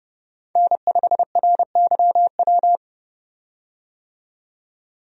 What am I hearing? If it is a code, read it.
Morse code, heard as a keyed tone, about 31 wpm, 700 Hz. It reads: N5RYW
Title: USA Call Signs - 5 Characters - 30wpm